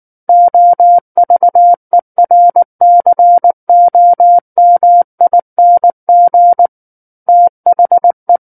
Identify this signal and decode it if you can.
Morse code, heard as a keyed tone, about 19 wpm, 700 Hz. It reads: OVERCOMING THE
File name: Book One / 10-Chapter 10 - 19 WPM